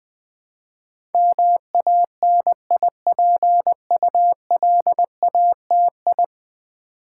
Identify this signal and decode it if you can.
Morse code, heard as a keyed tone, about 20 wpm, 700 Hz. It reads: MANIPULATI
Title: Build A Word - 5 Letters At A Time - 12-Letter Words - 20wpm